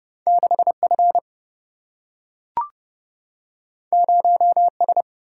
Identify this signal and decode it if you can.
Morse code, heard as a keyed tone, about 30 wpm, 700 Hz. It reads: 6 F 0 S
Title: Single-Letter Single-Number 30wpm